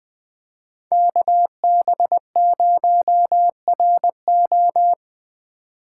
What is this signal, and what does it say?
Morse code, heard as a keyed tone, about 25 wpm, 700 Hz. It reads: KB0RO
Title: Speed Racing - USA 2x2 Call Signs 20wpm